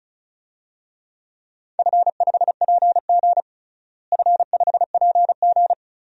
Morse code, heard as a keyed tone, about 36 wpm, 700 Hz. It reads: F5PG F5PG
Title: DXCC-100 Call Signs - 4 Characters - Repeated 2x - 35wpm